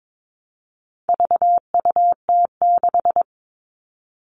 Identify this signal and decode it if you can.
Morse code, heard as a keyed tone, about 22 wpm, 700 Hz. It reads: VUT6
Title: All Cut Numbers - 4-Digit 22wpm